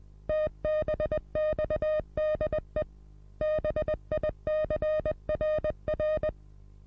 Morse code, heard as a keyed tone, about 20 wpm, 620 Hz. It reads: TBXDE BICRR
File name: repos / Morsekurs / DJ4UF Morsen27